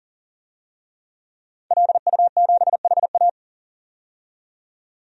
Morse code, heard as a keyed tone, about 40 wpm, 700 Hz. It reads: LU7HA
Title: DXCC-100 Call Signs - 5 Characters - 40wpm